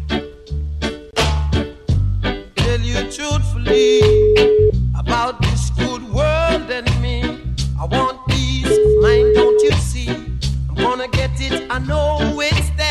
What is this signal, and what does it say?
Ringback tone (Europe, 425 Hz), with noise.